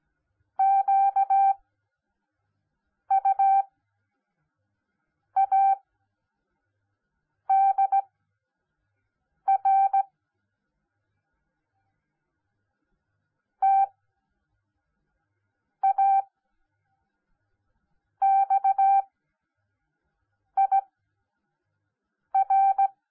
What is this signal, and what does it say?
Morse code, heard as a keyed tone, about 17 wpm, spaced as for 5 wpm, 780 Hz. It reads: QUADR TAXIR
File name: Onlinelehrgang / morsen / DJ4UF Morsen10